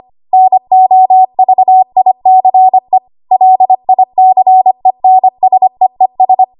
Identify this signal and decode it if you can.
Morse code, heard as a keyed tone, about 25 wpm, 750 Hz. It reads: NOVICE LICENSEES
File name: Archive / 25 WPM / 25WPM